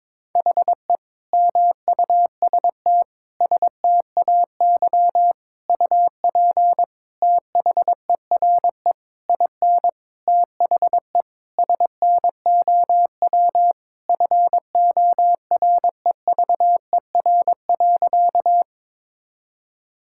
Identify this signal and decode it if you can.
Morse code, heard as a keyed tone, about 22 wpm, 700 Hz. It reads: HE MUST STAY UP THERE IN THE SNOW FOREVER.
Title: Sentences from Top 1600 Words 22wpm